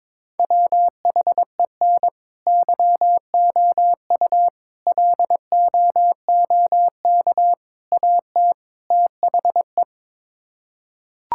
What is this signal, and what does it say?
Morse code, heard as a keyed tone, about 22 wpm, 700 Hz. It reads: WHEN YOU LOOK AT THE
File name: practice / 22wpm / Build A Phrase - 5-Word Phrases - 22wpm